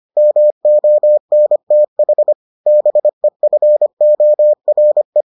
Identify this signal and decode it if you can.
Morse code, heard as a keyed tone, about 25 wpm, 600 Hz. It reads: MONTH BEFORE